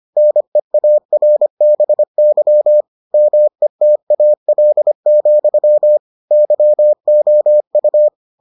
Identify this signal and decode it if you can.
Morse code, heard as a keyed tone, about 25 wpm, 600 Hz. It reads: NEARBY METAL, YOU